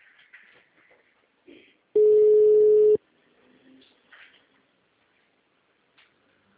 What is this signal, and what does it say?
Ringback tone (Europe, 425 Hz).
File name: out- -2020- - - .4